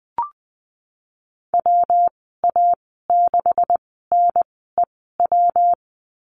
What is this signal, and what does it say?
Morse code, heard as a keyed tone, about 20 wpm, spaced as for 15 wpm, 700 Hz. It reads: WA6NEW
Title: 140 Lesson 23g - Letter f - Call Signs 20wpm